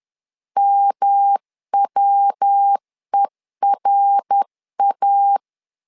Morse code, heard as a keyed tone, about 10 wpm, 800 Hz. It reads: MWERA
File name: LEZIONI / 3 MWRO / 3 MWR 10wpm 1952